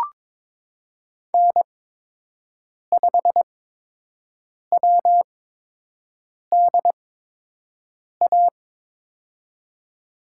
Morse code, heard as a keyed tone, about 22 wpm, spaced as for 6 wpm, 700 Hz. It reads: N5WDA
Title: USA Call Signs - 5 Characters - 8x Character Spacing - 22wpm